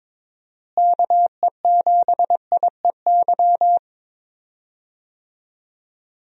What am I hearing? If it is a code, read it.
Morse code, heard as a keyed tone, about 22 wpm, 700 Hz. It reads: KE7IEY
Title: USA Call Signs - 6 Characters - Repeated 3x - 22wpm